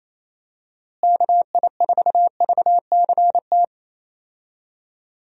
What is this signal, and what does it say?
Morse code, heard as a keyed tone, about 28 wpm, 700 Hz. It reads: KI4VCT
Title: USA Call Signs - 6 Characters - Repeated 3x - 28wpm